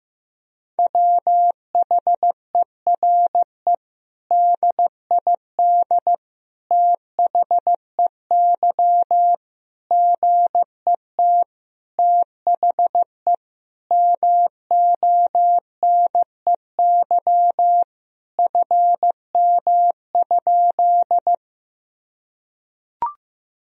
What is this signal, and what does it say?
Morse code, heard as a keyed tone, about 15 wpm, 700 Hz. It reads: WHERE DID THEY GET THE MONEY FM?
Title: Sentences from Top 400 Words and Abbreviations 15wpm